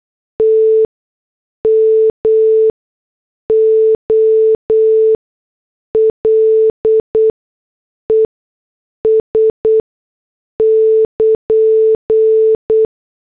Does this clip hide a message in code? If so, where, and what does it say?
Yes: Morse code from 0.4 to 12.8 s, heard as a keyed tone, about 8 wpm, spaced as for 6 wpm, 440 Hz. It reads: T MOLES <KN>